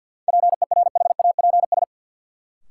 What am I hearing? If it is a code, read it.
Morse code, heard as a keyed tone, about 51 wpm, 700 Hz. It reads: PERHAPS
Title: Sentences from Top 500 Words - New Word Drill - 50wpm